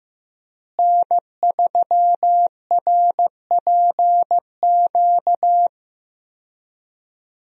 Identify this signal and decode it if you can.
Morse code, heard as a keyed tone, about 15 wpm, 700 Hz. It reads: N3RPQ
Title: USA Call Signs - 5 Characters - Repeated 3x - 15wpm